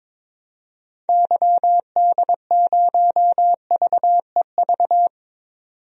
Morse code, heard as a keyed tone, about 22 wpm, 700 Hz. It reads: YD0VEV
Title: DXCC-100 Call Signs - 6 Characters - 22wpm